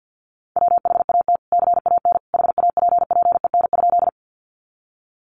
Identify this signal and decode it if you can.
Morse code, heard as a keyed tone, about 50 wpm, 700 Hz. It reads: WHAT CAN HAPPEN?
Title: Sentences from Top 400 Words 50wpm